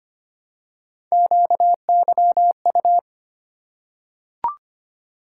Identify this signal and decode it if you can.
Morse code, heard as a keyed tone, about 25 wpm, 700 Hz. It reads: QYU
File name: Three Letter-Number Combinations - 25wpm